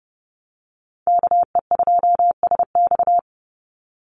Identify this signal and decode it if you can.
Morse code, heard as a keyed tone, about 30 wpm, 700 Hz. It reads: KE2SX
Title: Call Signs - 30wpm